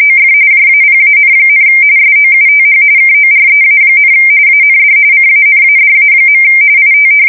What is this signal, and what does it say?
RTTY, about 45.45 baud. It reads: OF JAPAN, SOUTH KOREA, AND CANADA. IT
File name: fsk170 / USS-New-Jersey